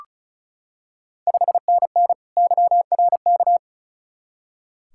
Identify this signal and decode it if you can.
Morse code, heard as a keyed tone, about 36 wpm, 700 Hz. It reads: 5NN YRK
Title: Ontario QSO Party Exchange 35wpm